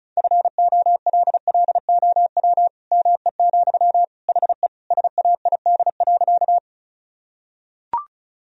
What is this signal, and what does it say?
Morse code, heard as a keyed tone, about 36 wpm, 700 Hz. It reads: FOLLOW ME, HE SAID.
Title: Sentences from Top 1000 Words - Review Part 1 - 35wpm